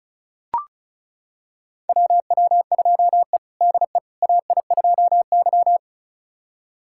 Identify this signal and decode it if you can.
Morse code, heard as a keyed tone, about 36 wpm, 700 Hz. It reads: WW2E DE AI2Y
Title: USA Callsign DE Callsign - 4 Characters - 35wpm